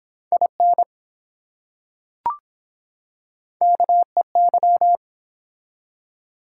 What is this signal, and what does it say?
Morse code, heard as a keyed tone, about 22 wpm, 700 Hz. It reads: IN KEY